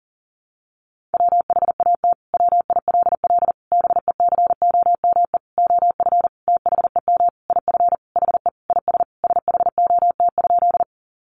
Morse code, heard as a keyed tone, about 40 wpm, 700 Hz. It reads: WHAT WILL BECOME OF THEM IF HE IS SHOT?
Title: Sentences from Top 1100 Words 40wpm